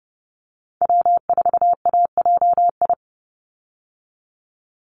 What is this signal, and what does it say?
Morse code, heard as a keyed tone, about 30 wpm, 700 Hz. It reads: W4AJI
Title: USA Call Signs 30wpm